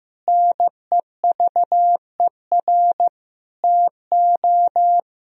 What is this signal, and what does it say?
Morse code, heard as a keyed tone, about 15 wpm, 700 Hz. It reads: NEVER TO
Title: Sentences from Top 1000 Words - Review Part 3 - 15wpm